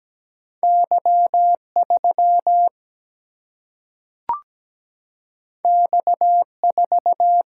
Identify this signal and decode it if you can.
Morse code, heard as a keyed tone, about 17 wpm, 700 Hz. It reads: Y 3 X 4
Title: Single-Letter Single-Number 17wpm